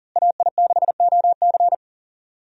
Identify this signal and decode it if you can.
Morse code, heard as a keyed tone, about 40 wpm, 700 Hz. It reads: AI6OC
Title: USA Call Signs - 5 Characters - Repeated 2x - 40wpm